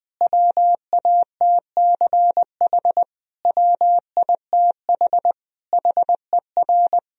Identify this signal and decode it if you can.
Morse code, heard as a keyed tone, about 20 wpm, 700 Hz. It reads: WATCH WITH HER
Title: Sentences from Top 1000 Words - Review Part 4 - 20wpm